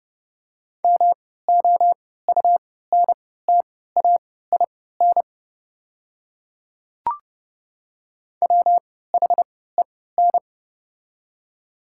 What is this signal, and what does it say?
Morse code, heard as a keyed tone, about 28 wpm, spaced as for 17 wpm, 690 Hz. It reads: MOUNTAIN WHEN
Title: 3x Character Spacing - Top 300 Words - 30wpm